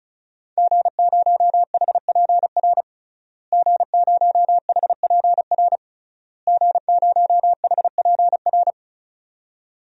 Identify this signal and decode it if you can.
Morse code, heard as a keyed tone, about 36 wpm, 700 Hz. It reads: G0HPR G0HPR G0HPR
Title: DXCC-100 Call Signs - 5 Characters - Repeated 3x - 35wpm